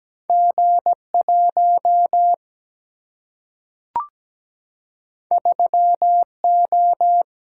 Morse code, heard as a keyed tone, about 17 wpm, 700 Hz. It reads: G1 3O
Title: Single-Letter Single-Number 17wpm